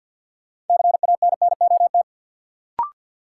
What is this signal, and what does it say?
Morse code, heard as a keyed tone, about 46 wpm, 700 Hz. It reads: CANNOT
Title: Sentences from Top 1000 Words - Review Part 6 - New Word Drill - 50wpm